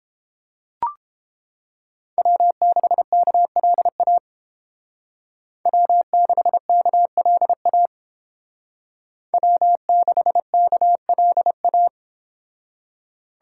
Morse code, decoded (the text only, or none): W6KLA W6KLA W6KLA